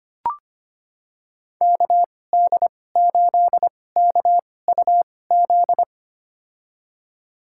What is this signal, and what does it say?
Morse code, heard as a keyed tone, about 25 wpm, spaced as for 18 wpm, 700 Hz. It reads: KD8KUZ
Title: USA Call Signs - 6 Characters - 2x Character Spacing - 25wpm